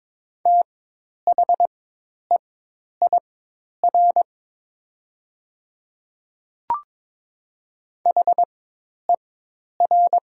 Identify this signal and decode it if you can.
Morse code, heard as a keyed tone, about 22 wpm, spaced as for 10 wpm, 700 Hz. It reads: THEIR HER
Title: 4x Character Spacing - Top 300 Words - 22wpm